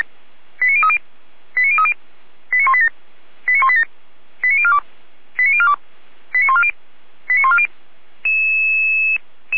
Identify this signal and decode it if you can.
Five-tone selective call (ZVEI), heard as quick runs of five tones, tones of about 70 ms. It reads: 89030 89030 89187 89187 89042 89042 89140 89140
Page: SDR/ZVEI